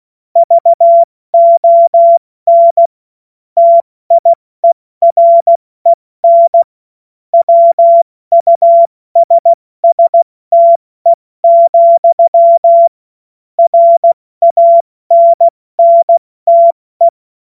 Morse code, heard as a keyed tone, about 16 wpm, spaced as for 14 wpm, 680 Hz. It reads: VON TIEREN WUSSTE, RANNTE
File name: Morsekurs / Witze / Witz 07. März 2023 70 F80